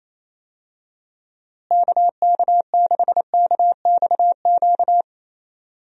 Morse code, heard as a keyed tone, about 28 wpm, 700 Hz. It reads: KK6KXQ
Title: USA Call Signs - 6 Characters - Repeated 3x - 28wpm